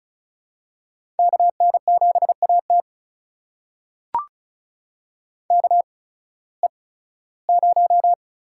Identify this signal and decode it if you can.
Morse code, heard as a keyed tone, about 36 wpm, spaced as for 19 wpm, 700 Hz. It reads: KN7AT KE0
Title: USA Call Signs - 8x Character Spacing - 35wpm